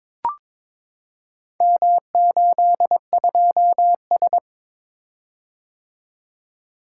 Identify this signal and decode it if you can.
Morse code, heard as a keyed tone, about 22 wpm, 700 Hz. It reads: M82S
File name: practice / 22wpm / Four Letter-Number Combinations - 22wpm